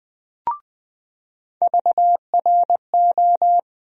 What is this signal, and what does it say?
Morse code, heard as a keyed tone, about 20 wpm, 700 Hz. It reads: VRO
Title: USA Call Sign Postfixes - 3 Letters 20wpm